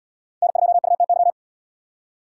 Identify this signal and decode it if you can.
Morse code, heard as a keyed tone, about 77 wpm, 700 Hz. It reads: N2UTY